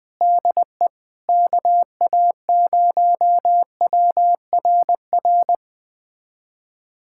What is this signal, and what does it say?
Morse code, heard as a keyed tone, about 20 wpm, 700 Hz. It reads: DE KA0WRR
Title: USA Callsign DE Callsign 20wpm